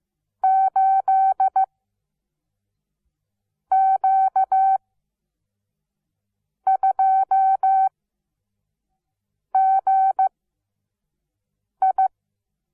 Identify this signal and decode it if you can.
Morse code, heard as a keyed tone, about 15 wpm, spaced as for 4 wpm, 770 Hz. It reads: 8Q2GI
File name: Onlinelehrgang / morsen / DJ4UF Morsen17